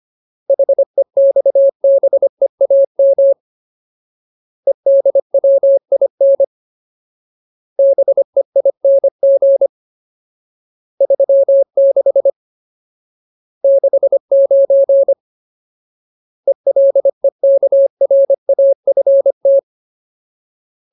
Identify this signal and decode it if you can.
Morse code, heard as a keyed tone, about 25 wpm, 550 Hz. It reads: HEXBEAM EDWIN BEING 36 69 ELEKRAFT
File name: Ultimate Word List / 25WPM Ultimate